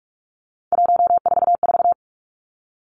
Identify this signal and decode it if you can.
Morse code, heard as a keyed tone, about 46 wpm, 700 Hz. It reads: 144
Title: Three Challenging Letter-Number Combinations 45wpm